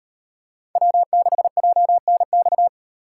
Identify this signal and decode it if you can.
Morse code, heard as a keyed tone, about 38 wpm, 700 Hz. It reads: W6JNX